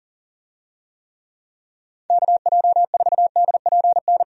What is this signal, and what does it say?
Morse code, heard as a keyed tone, about 40 wpm, 700 Hz. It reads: KJ4DPN
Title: USA Call Signs - 6 Characters - Repeated 3x - 40wpm